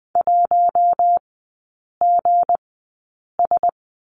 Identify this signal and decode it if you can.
Morse code, heard as a keyed tone, about 20 wpm, spaced as for 8 wpm, 700 Hz. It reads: 1GS
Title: 159 Lesson 26e - Letter g - Three Characters 20wpm